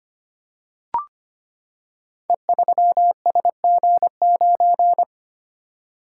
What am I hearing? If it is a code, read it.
Morse code, heard as a keyed tone, about 25 wpm, 700 Hz. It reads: E3SG9
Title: Code Groups - 5 Letter:Numbers - 25wpm